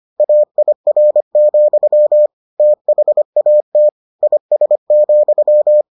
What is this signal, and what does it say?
Morse code, heard as a keyed tone, about 25 wpm, 600 Hz. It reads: AIR, THAT IS,